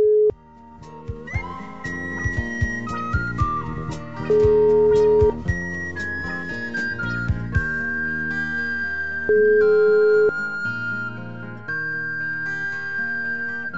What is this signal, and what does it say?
Ringback tone (Europe, 425 Hz), with noise.